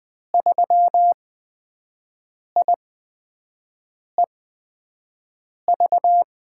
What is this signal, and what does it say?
Morse code, heard as a keyed tone, about 20 wpm, spaced as for 5 wpm, 700 Hz. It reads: 3IEV
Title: USA Call Signs - 5 Characters - 8x Character Spacing - 20wpm